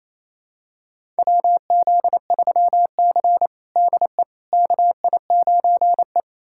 Morse code, heard as a keyed tone, about 28 wpm, 700 Hz. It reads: WZ3C DE KI9E
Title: USA Callsign DE Callsign - 4 Characters - 28wpm